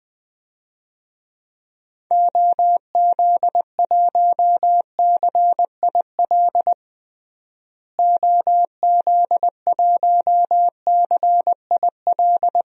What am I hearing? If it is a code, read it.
Morse code, heard as a keyed tone, about 20 wpm, 700 Hz. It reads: OZ1CIL OZ1CIL
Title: DXCC-100 Call Signs - 6 Characters - Repeated 2x - 20wpm